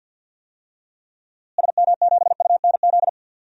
Morse code, heard as a keyed tone, about 52 wpm, 700 Hz. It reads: SM7UNZ
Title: DXCC-100 Call Signs - 6 Characters - Repeated 2x - 50wpm